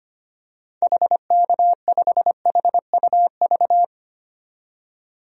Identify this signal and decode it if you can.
Morse code, heard as a keyed tone, about 25 wpm, 700 Hz. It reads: HK5HUV
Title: DXCC-100 Call Signs - 6 Characters - 25wpm